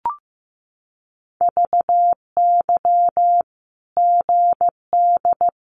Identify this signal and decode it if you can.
Morse code, heard as a keyed tone, about 15 wpm, 700 Hz. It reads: VY GD
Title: Common QSO Phrases 15wpm